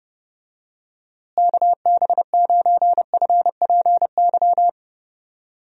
Morse code, heard as a keyed tone, about 30 wpm, 700 Hz. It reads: KB9FPY
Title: USA Call Signs - 6 Characters - Repeated 3x - 30wpm